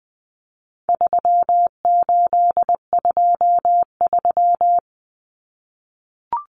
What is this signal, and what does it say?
Morse code, heard as a keyed tone, about 20 wpm, 700 Hz. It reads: 3823
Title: Common Cut Numbers - 4-Digit 20wpm